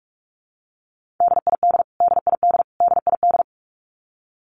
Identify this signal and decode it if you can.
Morse code, heard as a keyed tone, about 44 wpm, 700 Hz. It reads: DID DID DID
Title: Top 200 Words Repeated 45wpm